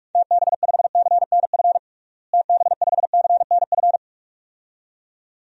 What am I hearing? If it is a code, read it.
Morse code, heard as a keyed tone, about 46 wpm, 700 Hz. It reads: TB5CNF TB5CNF
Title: DXCC-100 Call Signs - 6 Characters - Repeated 2x - 45wpm